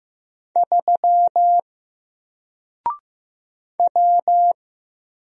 Morse code, heard as a keyed tone, about 15 wpm, 700 Hz. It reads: 3 W
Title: Single Character (Letter:Number:Punctuation:Prosign) Alt 15wpm